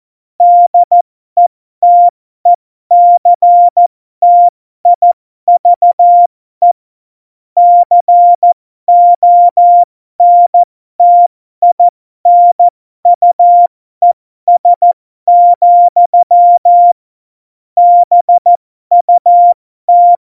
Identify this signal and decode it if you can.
Morse code, heard as a keyed tone, about 14 wpm, spaced as for 12 wpm, 700 Hz. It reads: DETECTIVE CONTINUES, BUT
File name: Morsekurs / Witze